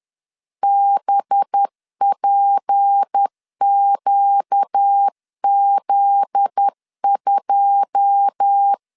Morse code, heard as a keyed tone, about 10 wpm, 800 Hz. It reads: BPQZ2
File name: LEZIONI / 9 2 10wpm 2736